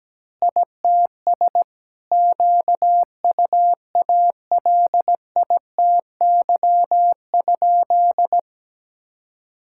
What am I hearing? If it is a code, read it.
Morse code, heard as a keyed tone, about 17 wpm, 700 Hz. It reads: ITS QUALITY?
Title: Sentences from Top 1000 Words - Review Part 6 - 17wpm